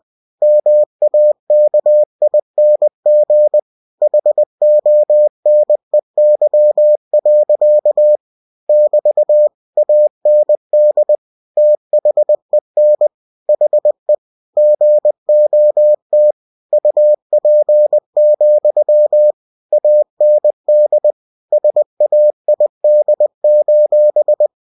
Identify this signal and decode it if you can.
Morse code, heard as a keyed tone, about 20 wpm, 600 Hz. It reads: MAKING HONEY. <BT> AND THEN HE GOT UP, AND SAID:
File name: cw-books / winnie-the-pooh / winnie-20wpm-page-0002